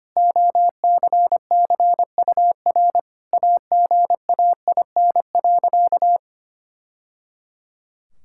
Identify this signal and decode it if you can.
Morse code, heard as a keyed tone, about 25 wpm, 700 Hz. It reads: OCCUR AGAIN.
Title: Sentences from Top 1000 Words - Review Part 6 - 25wpm